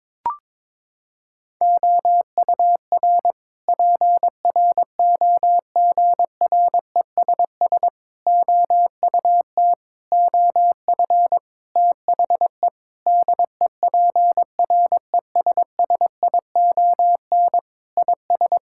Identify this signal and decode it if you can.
Morse code, heard as a keyed tone, about 22 wpm, 700 Hz. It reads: OUR PROGRESS OUT OF THE DEPRESSION IS
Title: Sentences from Top 1600 Words 22wpm